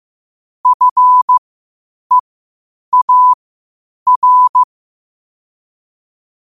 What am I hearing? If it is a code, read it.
Morse code, heard as a keyed tone, about 15 wpm, spaced as for 8 wpm, 1000 Hz. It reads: FEAR
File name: qotdfiles / 8wpm